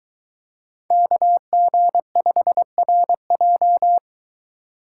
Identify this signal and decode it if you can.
Morse code, heard as a keyed tone, about 23 wpm, 700 Hz. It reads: KG5RJ